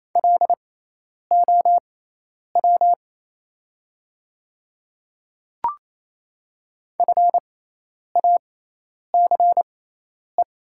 Morse code, heard as a keyed tone, about 28 wpm, spaced as for 10 wpm, 700 Hz. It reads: LOW FACE
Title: 6x Character Spacing - Top 500 Words - 28wpm